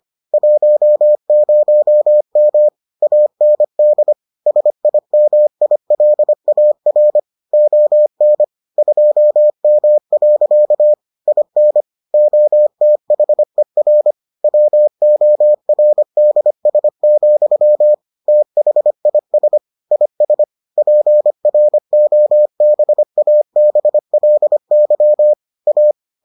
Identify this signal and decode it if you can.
Morse code, heard as a keyed tone, about 25 wpm, 600 Hz. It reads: 10M AND SIMILAR ON 2M. IN OTHER WORDS, THIS IS PROBABLY A